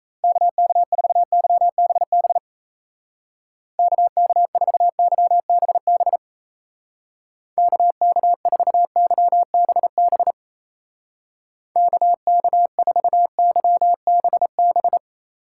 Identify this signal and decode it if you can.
Morse code, heard as a keyed tone, about 34 wpm, 700 Hz. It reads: KK4YBB KK4YBB KK4YBB KK4YBB
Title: Speed Racing - USA 2x3 Call Signs 28wpm